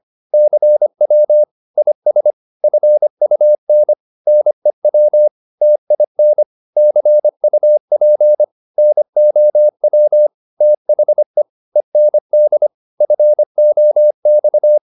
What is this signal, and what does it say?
Morse code, heard as a keyed tone, about 25 wpm, 600 Hz. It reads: CW IS FUN NEW TIN CUP NOW THE END FOX